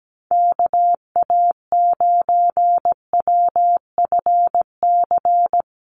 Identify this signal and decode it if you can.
Morse code, heard as a keyed tone, about 17 wpm, 700 Hz. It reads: KA9WFC
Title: USA Call Signs Repeated 2x 17wpm